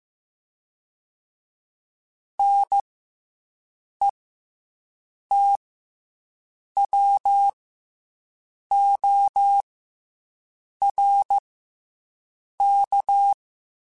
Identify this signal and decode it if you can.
Morse code, heard as a keyed tone, about 15 wpm, 800 Hz. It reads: N E T W O R K
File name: Special INGs / Special INGs-mp3s / SlowToFaster ing @13